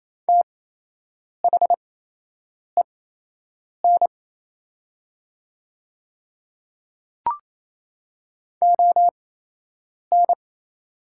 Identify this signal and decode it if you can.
Morse code, heard as a keyed tone, about 28 wpm, spaced as for 8 wpm, 700 Hz. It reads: THEN ON
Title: 8x Character Spacing - Top 100 Words - 28wpm